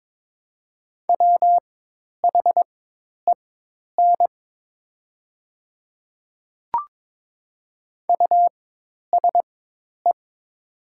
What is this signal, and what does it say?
Morse code, heard as a keyed tone, about 22 wpm, spaced as for 10 wpm, 700 Hz. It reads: WHEN USE
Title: 4x Character Spacing - Top 100 Words - 22wpm